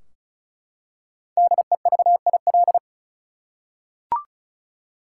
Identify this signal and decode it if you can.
Morse code, heard as a keyed tone, about 36 wpm, 700 Hz. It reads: DEVIL
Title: Five Letter Words - 35wpm